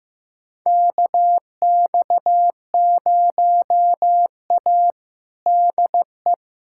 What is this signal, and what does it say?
Morse code, heard as a keyed tone, about 16 wpm, 700 Hz. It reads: KX0A DE
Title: USA Callsign DE Callsign 15wpm